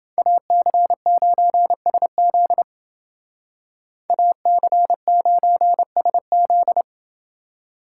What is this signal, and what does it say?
Morse code, heard as a keyed tone, about 28 wpm, 700 Hz. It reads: AC9SZ AC9SZ